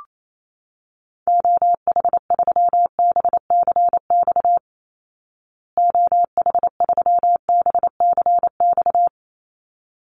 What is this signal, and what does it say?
Morse code, heard as a keyed tone, about 28 wpm, 700 Hz. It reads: OH3BCX OH3BCX
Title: Call Signs Repeated 3x - 28wpm